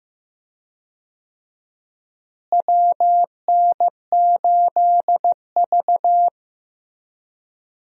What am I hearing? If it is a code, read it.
Morse code, heard as a keyed tone, about 15 wpm, 700 Hz. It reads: WN8V
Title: USA Call Signs - 4 Characters - Repeated 2x - 15wpm